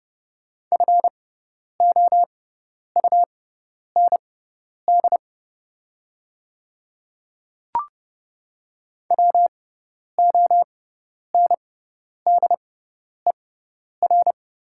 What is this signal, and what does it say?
Morse code, heard as a keyed tone, about 31 wpm, spaced as for 10 wpm, 690 Hz. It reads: FOUND WONDER
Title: 6x Character Spacing - Top 500 Words - 30wpm